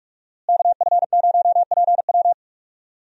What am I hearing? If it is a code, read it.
Morse code, heard as a keyed tone, about 47 wpm, 700 Hz. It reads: KF0PW
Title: USA Call Signs - 5 Characters - Repeated 3x - 45wpm